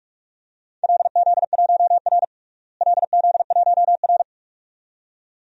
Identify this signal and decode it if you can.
Morse code, heard as a keyed tone, about 42 wpm, 700 Hz. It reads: LZ1R LZ1R
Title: DXCC-100 Call Signs - 4 Characters - Repeated 2x - 45wpm